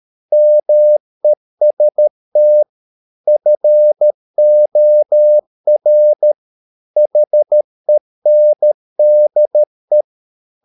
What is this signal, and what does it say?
Morse code, heard as a keyed tone, about 13 wpm, 600 Hz. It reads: MEST FOR HENDE
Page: Ferie 13 wpm